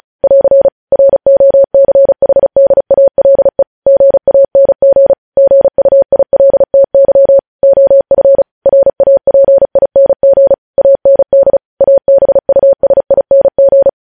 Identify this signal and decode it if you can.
Morse code, heard as a keyed tone, about 36 wpm, 550 Hz. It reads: <AR> ROCHDALE GANG GUILTY OF RAPING AND ABUSING
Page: news headlines in morse code at 35 wpm